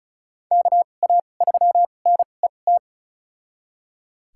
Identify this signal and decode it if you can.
Morse code, heard as a keyed tone, about 36 wpm, spaced as for 25 wpm, 700 Hz. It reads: KA3NET
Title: USA Call Signs - 6 Characters - 2x Character Spacing - 35wpm